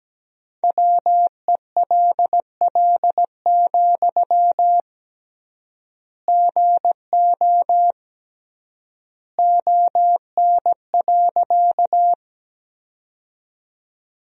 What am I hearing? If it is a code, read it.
Morse code, heard as a keyed tone, about 17 wpm, 700 Hz. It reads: WELL, GO ON.
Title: 3x Word-Spacing - Sentences from Top 200 Words - 17wpm